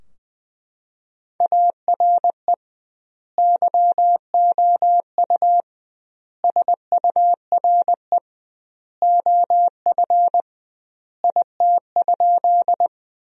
Morse code, heard as a keyed tone, about 20 wpm, 700 Hz. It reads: ARE YOU SURE OF IT?
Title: 2x Word-Spacing - Sentences from Top 300 Words - 20wpm